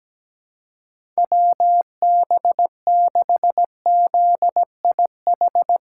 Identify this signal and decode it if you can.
Morse code, heard as a keyed tone, about 17 wpm, 700 Hz. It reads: WB6ZIH
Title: USA Call Signs - 6 Characters - 4x Character Spacing - 17wpm